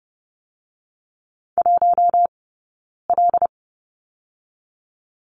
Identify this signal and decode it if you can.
Morse code, heard as a keyed tone, about 30 wpm, 700 Hz. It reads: 1 L
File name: Lesson 18d - Letter m - Two Characters 30wpm